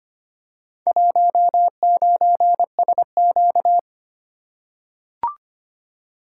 Morse code, heard as a keyed tone, about 25 wpm, 700 Hz. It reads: 19SQ
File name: practice / 25wpm / Four Letter-Number Combinations - 25wpm